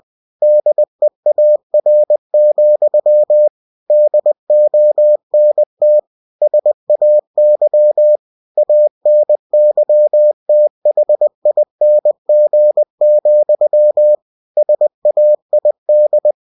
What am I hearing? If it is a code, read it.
Morse code, heard as a keyed tone, about 20 wpm, 600 Hz. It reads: DEAR, DONT SAY ANYTHING, SAID
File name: cw-books / winnie-the-pooh / winnie-20wpm-page-0072